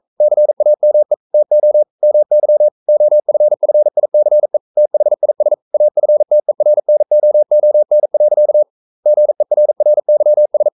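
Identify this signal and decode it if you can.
Morse code, heard as a keyed tone, about 42 wpm, 600 Hz. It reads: CAME TO MY OFFICE THIS AFTERNOON. GERRYS
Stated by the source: Morse code practice at 41 wpm, from CW med Gnister.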